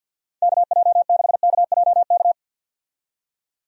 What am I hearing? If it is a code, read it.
Morse code, heard as a keyed tone, about 51 wpm, 700 Hz. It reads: KJ6KJK
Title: USA Call Signs - 6 Characters - 50wpm